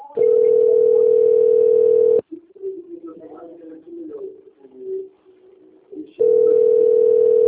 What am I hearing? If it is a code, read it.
Ringback tone (North America), 440 + 480 Hz.
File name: rg-1600-unknown- - - .14374